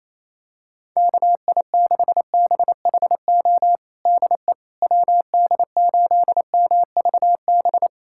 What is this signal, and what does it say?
Morse code, heard as a keyed tone, about 28 wpm, 700 Hz. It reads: KI6BHO DE WD8MVB
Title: USA Callsign DE Callsign 28wpm